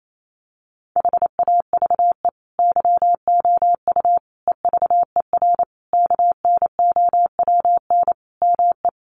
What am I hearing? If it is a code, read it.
Morse code, heard as a keyed tone, about 28 wpm, 700 Hz. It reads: HAVE YOU EVER KNOWN ME
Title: Sentences from Top 1400 Words 28wpm